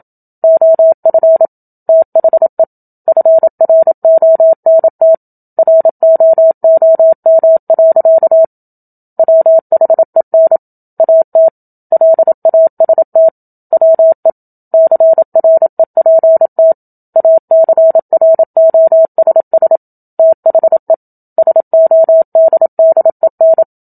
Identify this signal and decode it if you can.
Morse code, heard as a keyed tone, about 27 wpm, 650 Hz. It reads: OF THE FRONT ROOM. WHEN AT LAST WE CREPT ACROSS THE SODDEN
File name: Book Two / 01-Chapter 1 - 27 WPM